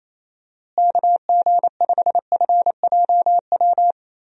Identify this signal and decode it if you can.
Morse code, heard as a keyed tone, about 27 wpm, 700 Hz. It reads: KG5FJW
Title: USA Call Signs - 6 Characters - Repeated 2x - 28wpm